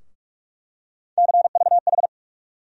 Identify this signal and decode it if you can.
Morse code, heard as a keyed tone, about 46 wpm, 700 Hz. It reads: CVH
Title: USA Call Sign Postfixes - 3 Letters 45wpm